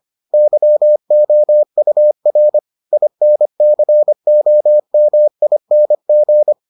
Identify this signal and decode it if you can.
Morse code, heard as a keyed tone, about 25 wpm, 600 Hz. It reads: YOUR INCOMING